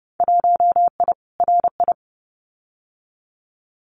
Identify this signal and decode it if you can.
Morse code, heard as a keyed tone, about 30 wpm, 700 Hz. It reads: 1I RI
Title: Winter Field Day Exchanges Encore 30wpm